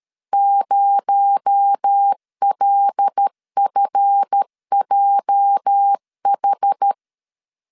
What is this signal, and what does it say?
Morse code, heard as a keyed tone, about 13 wpm, 800 Hz. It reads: 0LFJH
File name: LEZIONI / 8 YJ45 / 8 YJ 12wpm 2235